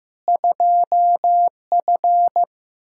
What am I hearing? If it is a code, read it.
Morse code, heard as a keyed tone, about 15 wpm, 700 Hz. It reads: 2F
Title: Practice QSO - Exchange 2 - Info - 15wpm